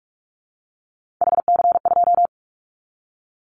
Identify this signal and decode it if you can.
Morse code, heard as a keyed tone, about 44 wpm, 700 Hz. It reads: HC2
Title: Callsign Prefixes for Top 100 DXCC - 45wpm